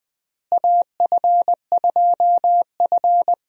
Morse code, heard as a keyed tone, about 20 wpm, 700 Hz. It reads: AF2F
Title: USA Call Signs - 4 Characters - Repeated 3x - 20wpm